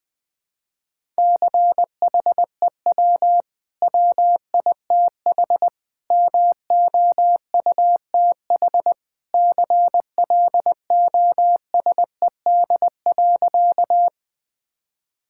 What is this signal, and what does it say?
Morse code, heard as a keyed tone, about 20 wpm, 700 Hz. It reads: CHEW WITH MOUTH CLOSED.
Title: Easy Sentences with Hints - 20wpm